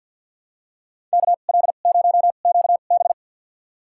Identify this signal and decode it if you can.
Morse code, heard as a keyed tone, about 46 wpm, spaced as for 34 wpm, 700 Hz. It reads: KL0QB
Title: USA Call Signs - 5 Characters - 2x Character Spacing - 50wpm